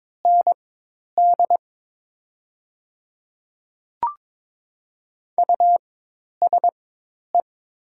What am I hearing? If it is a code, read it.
Morse code, heard as a keyed tone, about 22 wpm, spaced as for 10 wpm, 700 Hz. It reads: ND USE
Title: 4x Character Spacing - Top 100 Words - 22wpm